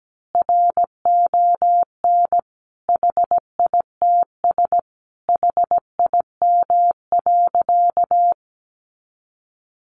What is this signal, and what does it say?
Morse code, heard as a keyed tone, about 17 wpm, 700 Hz. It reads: RON HITS HIM.
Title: Additional Easy Phrases 17wpm